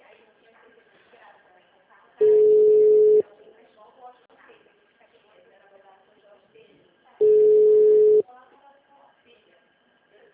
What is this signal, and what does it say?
Ringback tone (Europe, 425 Hz), with speech.
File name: out- -2015- - - .68797